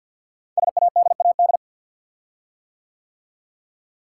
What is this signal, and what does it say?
Morse code, heard as a keyed tone, about 50 wpm, 690 Hz. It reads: SADAD
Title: Code Groups - 5 Letters - 50wpm